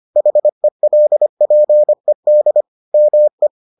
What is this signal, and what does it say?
Morse code, heard as a keyed tone, about 25 wpm, 600 Hz. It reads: HELPED ME